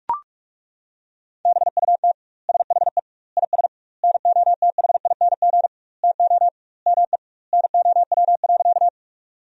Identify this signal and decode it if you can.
Morse code, heard as a keyed tone, about 41 wpm, 700 Hz. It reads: BUT SHE IS NOTHING TO ME NOW.
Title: 2x Word-Spacing - Sentences from Top 500 Words - 45wpm